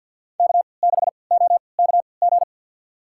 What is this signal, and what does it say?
Morse code, heard as a keyed tone, about 51 wpm, spaced as for 28 wpm, 700 Hz. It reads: K6OKG
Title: USA Call Signs - 3x Character Spacing - 50wpm